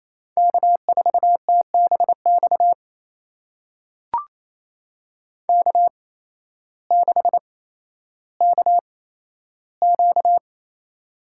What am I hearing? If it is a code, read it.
Morse code, heard as a keyed tone, about 28 wpm, spaced as for 13 wpm, 700 Hz. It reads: K4TBX K6KQ
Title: USA Call Signs - 8x Character Spacing - 28wpm